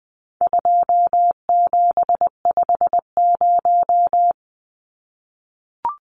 Morse code, heard as a keyed tone, about 20 wpm, 700 Hz. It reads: 2750
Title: Numbers - 4-Digit 20wpm